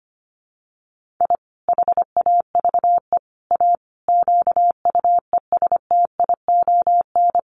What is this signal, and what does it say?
Morse code, heard as a keyed tone, about 25 wpm, 700 Hz. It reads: I HAVE A QUESTION
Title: Sets of 5 Words Encore2 25wpm